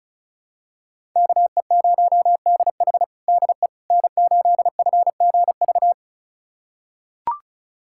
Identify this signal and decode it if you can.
Morse code, heard as a keyed tone, about 36 wpm, 700 Hz. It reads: KE0DH DE N8FGV
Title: USA Callsign DE Callsign - 5 Characters - 35wpm